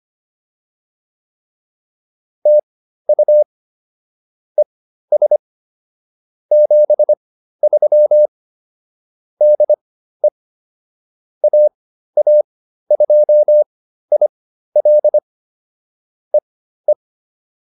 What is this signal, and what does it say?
Morse code, heard as a keyed tone, about 25 wpm, spaced as for 13 wpm, 600 Hz. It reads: TU ES 73 DE AA2IL E E